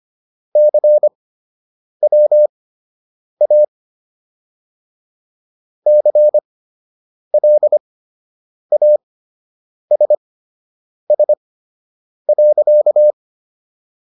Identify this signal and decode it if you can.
Morse code, heard as a keyed tone, about 25 wpm, spaced as for 8 wpm, 600 Hz. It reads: CWA CLASS.